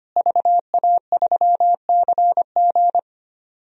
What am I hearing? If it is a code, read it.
Morse code, heard as a keyed tone, about 25 wpm, 700 Hz. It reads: VA3CG
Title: DXCC-100 Call Signs - 5 Characters - Repeated 2x - 25wpm